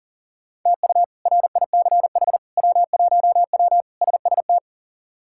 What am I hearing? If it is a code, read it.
Morse code, heard as a keyed tone, about 37 wpm, 700 Hz. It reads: TU RICH W1W SST
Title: K1USN SST QSO Practice - 40wpm